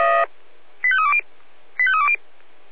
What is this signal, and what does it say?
Five-tone selective call (ZVEI), heard as quick runs of five tones, tones of about 70 ms. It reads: 85320 85320
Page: /496/SDR/ZVEI/85310/2025.07.02/